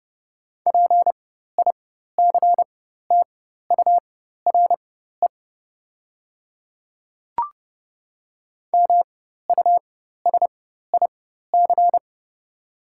Morse code, heard as a keyed tone, about 31 wpm, spaced as for 14 wpm, 700 Hz. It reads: PICTURE MUSIC
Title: 4x Character Spacing - Top 300 Words - 30wpm